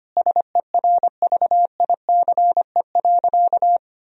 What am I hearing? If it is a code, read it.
Morse code, heard as a keyed tone, about 25 wpm, 700 Hz. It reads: SERVICE.
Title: Sentences from Top 1000 Words - Review Part 4 - 25wpm